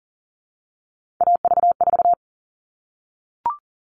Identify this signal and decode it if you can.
Morse code, heard as a keyed tone, about 40 wpm, 700 Hz. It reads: AV4
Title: All Cut Numbers - 3-Digit 40wpm